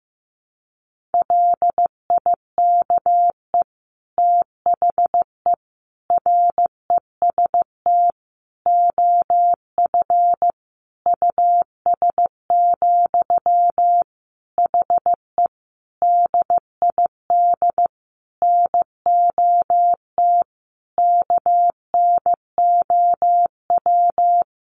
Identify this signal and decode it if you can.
Morse code, heard as a keyed tone, about 15 wpm, 700 Hz. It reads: LIKE THE REST OF US, HE DID NOT KNOW
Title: Sentences from Top 1400 Words 15wpm